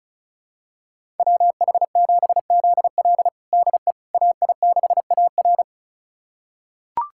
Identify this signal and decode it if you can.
Morse code, heard as a keyed tone, about 36 wpm, 700 Hz. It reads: WH7ZL DE AI6AR
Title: USA Callsign DE Callsign - 5 Characters - 35wpm